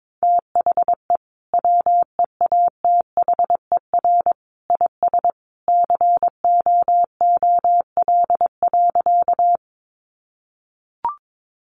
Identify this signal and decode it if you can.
Morse code, heard as a keyed tone, about 22 wpm, 700 Hz. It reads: THE WEATHER IS COOL.
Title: Sentences from Top 1200 Words 22wpm